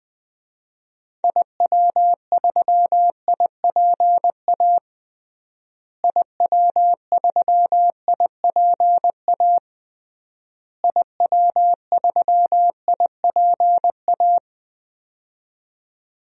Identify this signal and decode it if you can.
Morse code, heard as a keyed tone, about 20 wpm, 700 Hz. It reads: IW3IPA IW3IPA IW3IPA
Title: DXCC-100 Call Signs - 6 Characters - Repeated 3x - 20wpm